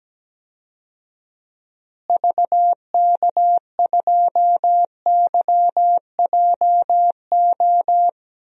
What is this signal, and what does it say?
Morse code, heard as a keyed tone, about 17 wpm, 700 Hz. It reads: VK2YJO
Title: DXCC-100 Call Signs - 6 Characters - Repeated 2x - 17wpm